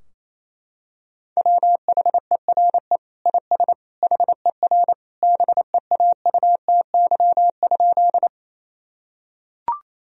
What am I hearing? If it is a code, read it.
Morse code, heard as a keyed tone, about 28 wpm, 700 Hz. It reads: WHERE IS HER BEAUTY?
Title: Sentences from Top 1000 Words - Review Part 2 - 28wpm